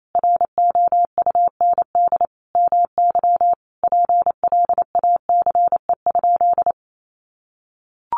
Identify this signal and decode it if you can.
Morse code, heard as a keyed tone, about 28 wpm, 700 Hz. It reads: ROUND MY PLACE?
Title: Sentences from Top 200 Words 28wpm